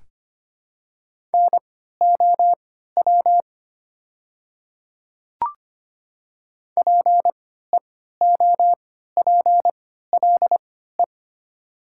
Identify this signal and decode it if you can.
Morse code, heard as a keyed tone, about 25 wpm, spaced as for 14 wpm, 700 Hz. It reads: NOW PEOPLE
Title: 3x Character Spacing - Top 300 Words - 25wpm